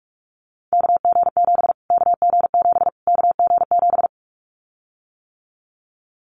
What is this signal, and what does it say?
Morse code, heard as a keyed tone, about 44 wpm, 700 Hz. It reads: KG7 KG7 KG7
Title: Callsign Prefixes for Top 25 DXCC - Repeated 3x - 45wpm